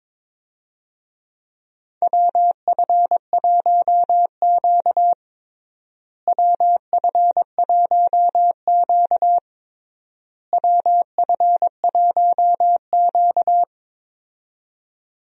Morse code, heard as a keyed tone, about 22 wpm, 700 Hz. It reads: WF1Q WF1Q WF1Q
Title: USA Call Signs - 4 Characters - Repeated 3x - 22wpm